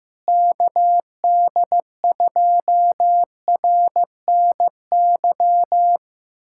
Morse code, heard as a keyed tone, about 16 wpm, 700 Hz. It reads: KD2RNY
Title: USA Callsign DE Callsign 15wpm